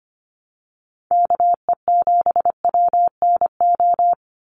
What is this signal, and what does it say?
Morse code, heard as a keyed tone, about 25 wpm, 700 Hz. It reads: KE7WNO
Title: USA Call Signs 25wpm